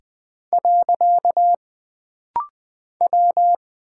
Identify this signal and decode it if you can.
Morse code, heard as a keyed tone, about 20 wpm, 700 Hz. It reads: . W
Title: Single Character (Letter:Number:Punctuation:Prosign) - Rapid-Fire - 20wpm